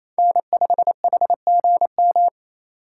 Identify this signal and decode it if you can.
Morse code, heard as a keyed tone, about 28 wpm, 700 Hz. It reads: N5HGM
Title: USA Call Signs - 5 Characters - 8x Character Spacing - 28wpm